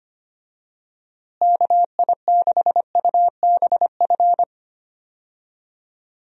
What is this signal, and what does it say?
Morse code, heard as a keyed tone, about 25 wpm, 700 Hz. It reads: KI6UBF
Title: USA Call Signs - 6 Characters - Repeated 2x - 25wpm